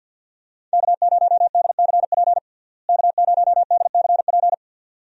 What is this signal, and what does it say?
Morse code, heard as a keyed tone, about 52 wpm, 700 Hz. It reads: K0DCP K0DCP
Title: USA Call Signs - 5 Characters - Repeated 2x - 50wpm